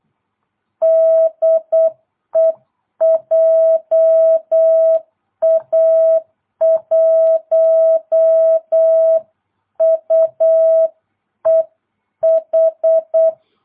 Morse code, heard as a keyed tone, about 8 wpm, 650 Hz. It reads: DE JA1UEH